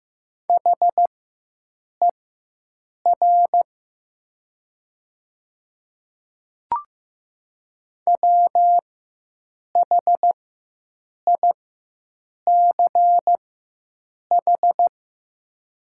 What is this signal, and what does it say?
Morse code, heard as a keyed tone, about 15 wpm, spaced as for 7 wpm, 700 Hz. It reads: HER WHICH
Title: 4x Character Spacing - Top 100 Words - 15wpm